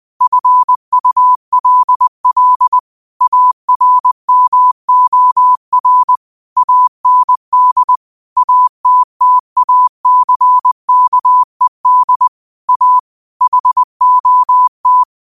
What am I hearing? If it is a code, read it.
Morse code, heard as a keyed tone, about 20 wpm, 1000 Hz. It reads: FULL ARMOR AND ATTACKED A HOT
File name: qotdfiles / 20wpm